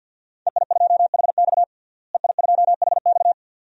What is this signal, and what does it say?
Morse code, heard as a keyed tone, about 47 wpm, 700 Hz. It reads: EI2HX EI2HX
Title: DXCC-100 Call Signs - 5 Characters - Repeated 2x - 50wpm